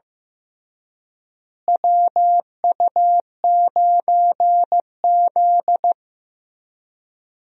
Morse code, heard as a keyed tone, about 15 wpm, 700 Hz. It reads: WU9Z
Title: USA Call Signs - 4 Characters - Repeated 3x - 15wpm